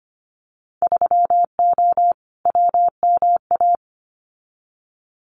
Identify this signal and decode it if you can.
Morse code, heard as a keyed tone, about 25 wpm, 700 Hz. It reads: 3O WMA
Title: Winter Field Day Exchanges Encore 25wpm